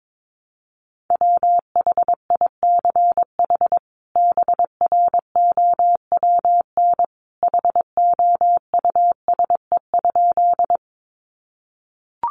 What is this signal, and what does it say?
Morse code, heard as a keyed tone, about 22 wpm, 700 Hz. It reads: WHICH BROWN HOUSE?
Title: Sentences from Top 1200 Words 22wpm